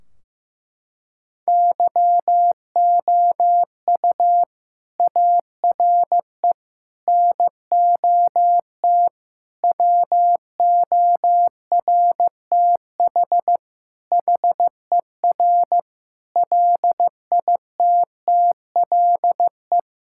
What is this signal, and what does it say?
Morse code, heard as a keyed tone, about 15 wpm, 700 Hz. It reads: YOU ARE NOT WORTH HER LITTLE
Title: Sentences from Top 1600 Words 15wpm